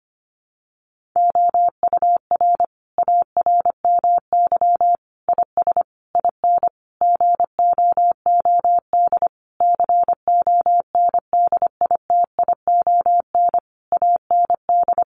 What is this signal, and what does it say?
Morse code, heard as a keyed tone, about 25 wpm, 700 Hz. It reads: OUR ARMY IS IN GOOD CONDITION AND
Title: Sentences from Top 1500 Words 25wpm